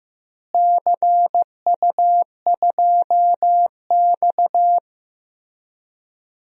Morse code, heard as a keyed tone, about 16 wpm, 700 Hz. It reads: CU2X
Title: DXCC-100 Call Signs - 4 Characters - Repeated 3x - 15wpm